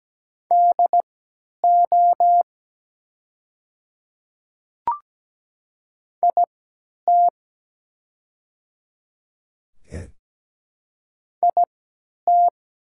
Morse code, heard as a keyed tone, about 17 wpm, spaced as for 10 wpm, 700 Hz. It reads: DO IT IT
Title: 3x Character Spacing - Top 100 Words - 17wpm